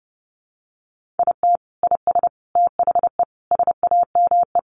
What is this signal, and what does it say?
Morse code, heard as a keyed tone, about 30 wpm, 700 Hz. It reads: IT IS THE SAME
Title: Sentences from Top 200 Words 30wpm